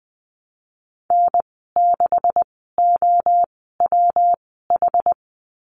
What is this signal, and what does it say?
Morse code, heard as a keyed tone, about 20 wpm, spaced as for 15 wpm, 700 Hz. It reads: N6OWH
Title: 167 Lesson 27g - Number 7 - Call Signs 20wpm